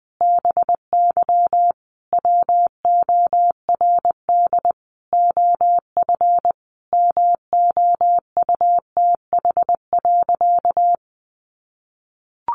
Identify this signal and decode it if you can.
Morse code, heard as a keyed tone, about 20 wpm, 700 Hz. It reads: BY WORD OF MOUTH.
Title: Sentences from Top 1500 Words 20wpm